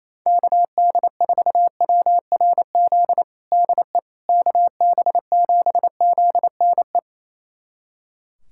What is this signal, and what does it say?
Morse code, heard as a keyed tone, about 28 wpm, 700 Hz. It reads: KD4WRZ DE KB7ZNE
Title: USA Callsign DE Callsign - 6 Characters - 28wpm